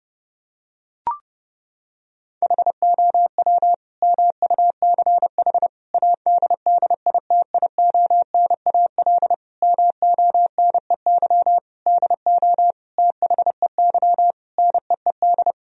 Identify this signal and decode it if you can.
Morse code, heard as a keyed tone, about 30 wpm, 700 Hz. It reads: HOW MUCH ADDITIONAL MONEY DO THEY NEED
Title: Sentences from Top 1000 Words - Review Part 4 - 30wpm